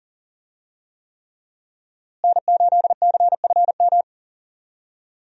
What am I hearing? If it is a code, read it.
Morse code, heard as a keyed tone, about 37 wpm, 700 Hz. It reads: N8CFM
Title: USA Call Signs - 5 Characters - Repeated 3x - 40wpm